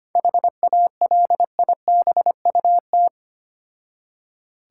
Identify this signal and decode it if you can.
Morse code, heard as a keyed tone, about 25 wpm, 700 Hz. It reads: HALIBUT
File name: practice / 25wpm / Difficult Words with Hints - 25wpm